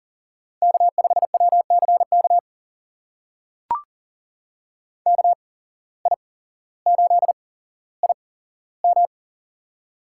Morse code, heard as a keyed tone, about 42 wpm, spaced as for 17 wpm, 700 Hz. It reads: K5WCK KI8IM
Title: USA Call Signs - 5 Characters - 8x Character Spacing - 40wpm